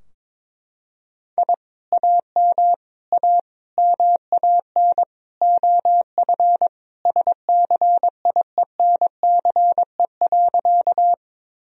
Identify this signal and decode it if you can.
Morse code, heard as a keyed tone, about 22 wpm, 700 Hz. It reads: I AM A MAN OF SCIENCE.
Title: Sentences from Top 1000 Words - Review Part 2 - 22wpm